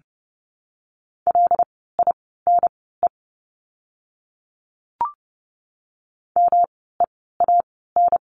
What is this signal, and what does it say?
Morse code, heard as a keyed tone, about 30 wpm, spaced as for 17 wpm, 700 Hz. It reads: LINE MEAN
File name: course / 30:10wpm / 115 Lesson 19f - Letter w - Words 30wpm